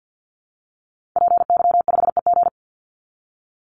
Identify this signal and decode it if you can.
Morse code, heard as a keyed tone, about 50 wpm, 700 Hz. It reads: PY5EG
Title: Call Signs Repeated 2x - 50wpm